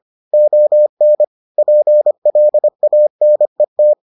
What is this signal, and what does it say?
Morse code, heard as a keyed tone, about 25 wpm, 600 Hz. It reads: ON PLANET